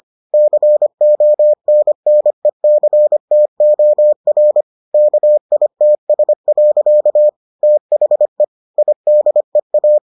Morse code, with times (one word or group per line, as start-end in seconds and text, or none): 0.3-4.6 CONNECTOR
4.9-7.3 KITS.
7.6-8.4 THE
8.8-10.0 IDEA